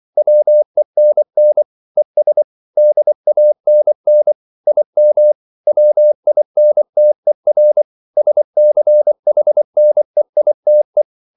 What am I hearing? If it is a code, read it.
Morse code, heard as a keyed tone, about 24 wpm, 600 Hz. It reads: WENN ES DANN IM WINTER SCHNEITE